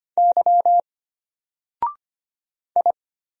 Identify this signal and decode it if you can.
Morse code, heard as a keyed tone, about 25 wpm, 700 Hz. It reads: Y I
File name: Single-Letter-Number-25wpm